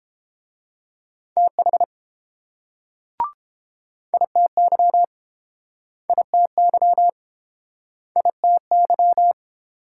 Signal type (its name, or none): Morse code